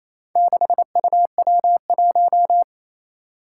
Morse code, heard as a keyed tone, about 29 wpm, 700 Hz. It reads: 6UW1
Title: Four Challenging Letter-Number Combinations 28wpm